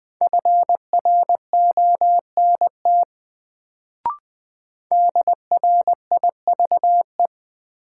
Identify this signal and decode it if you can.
Morse code, heard as a keyed tone, about 20 wpm, 700 Hz. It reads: FRONT DRIVE
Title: Sentences from Top 500 Words - New Word Drill - 20wpm